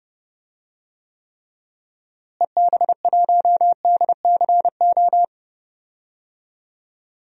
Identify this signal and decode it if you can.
Morse code, heard as a keyed tone, about 31 wpm, 700 Hz. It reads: EB1DCO
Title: DXCC-100 Call Signs - 6 Characters - Repeated 3x - 30wpm